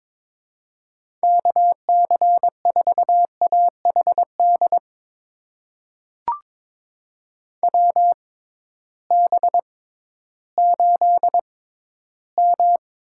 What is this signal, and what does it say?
Morse code, heard as a keyed tone, about 22 wpm, spaced as for 13 wpm, 700 Hz. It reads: KC4AHD WB8M
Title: USA Call Signs - 6 Characters - 6x Character Spacing - 22wpm